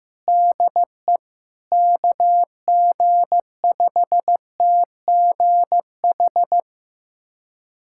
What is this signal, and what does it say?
Morse code, heard as a keyed tone, about 15 wpm, 700 Hz. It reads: DE KG5TGH
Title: USA Callsign DE Callsign - 6 Characters - 15wpm